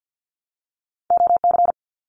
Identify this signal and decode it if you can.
Morse code, heard as a keyed tone, about 50 wpm, 700 Hz. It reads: OC
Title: Abbreviations Repeated 3x - 50wpm